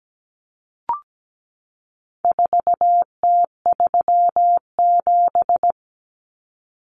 Morse code, heard as a keyed tone, about 17 wpm, 700 Hz. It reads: 4T37
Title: Common Cut Numbers - 4-Digit 17wpm